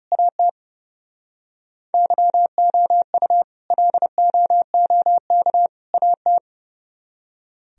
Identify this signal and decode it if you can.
Morse code, heard as a keyed tone, about 37 wpm, 700 Hz. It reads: AT YOU LOOK AT
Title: Speed-Racing - Sets of 3 Words 30wpm